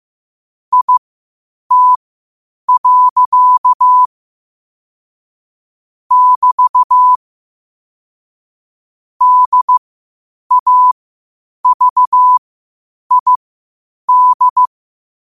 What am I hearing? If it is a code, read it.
Morse code, heard as a keyed tone, about 15 wpm, spaced as for 8 wpm, 1000 Hz. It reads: IT. <BT> DAVID
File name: qotdfiles / 8wpm